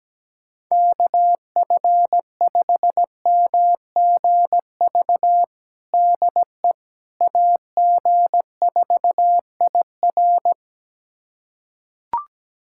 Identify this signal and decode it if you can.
Morse code, heard as a keyed tone, about 17 wpm, 700 Hz. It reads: KF5MGV DE AG4IR
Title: USA Callsign DE Callsign 17wpm